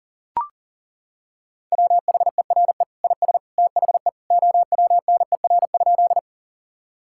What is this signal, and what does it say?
Morse code, heard as a keyed tone, about 40 wpm, 700 Hz. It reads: WHERE IS THE OWNER?
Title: Sentences from Top 1600 Words 40wpm